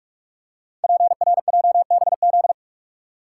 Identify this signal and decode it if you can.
Morse code, heard as a keyed tone, about 44 wpm, 700 Hz. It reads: PRJBZ
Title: Code Groups - 5 Letters - 45wpm